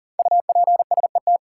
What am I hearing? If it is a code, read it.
Morse code, heard as a keyed tone, about 40 wpm, 700 Hz. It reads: UPSET
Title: Sentences from Top 800 Words - New Word Drill - 40wpm